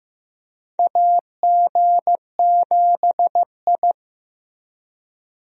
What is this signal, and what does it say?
Morse code, heard as a keyed tone, about 15 wpm, 700 Hz. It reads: AG7I
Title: USA Call Signs - 4 Characters - Repeated 3x - 15wpm